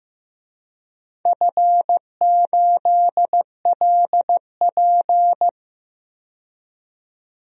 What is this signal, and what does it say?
Morse code, heard as a keyed tone, about 15 wpm, 700 Hz. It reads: F8LP
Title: DXCC-100 Call Signs - 4 Characters - 15wpm